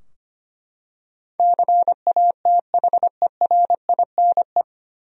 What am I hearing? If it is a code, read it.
Morse code, heard as a keyed tone, about 25 wpm, 700 Hz. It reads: CATHERINE
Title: USA Common Names - 25wpm